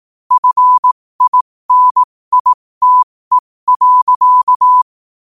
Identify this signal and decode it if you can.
Morse code, heard as a keyed tone, about 18 wpm, spaced as for 15 wpm, 1000 Hz. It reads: FINITE.
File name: qotdfiles / 15wpm